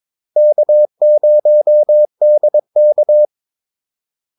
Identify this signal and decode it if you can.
Morse code, heard as a keyed tone, about 22 wpm, 600 Hz. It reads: K0DK
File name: Callsigns-22wpm